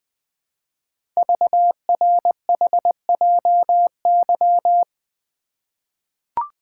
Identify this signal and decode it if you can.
Morse code, heard as a keyed tone, about 20 wpm, 700 Hz. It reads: VRHJY
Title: Code Groups - 5 Letter:Numbers - 20wpm